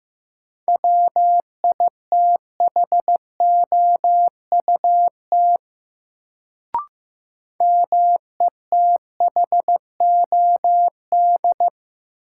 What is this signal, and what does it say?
Morse code, heard as a keyed tone, about 15 wpm, 700 Hz. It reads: WITHOUT METHOD
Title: Sentences from Top 600 Words - New Word Drill - 15wpm